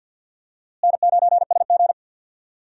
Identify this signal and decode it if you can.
Morse code, heard as a keyed tone, about 48 wpm, 700 Hz. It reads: N9SG
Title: USA Call Signs - 4 Characters - Repeated 2x - 50wpm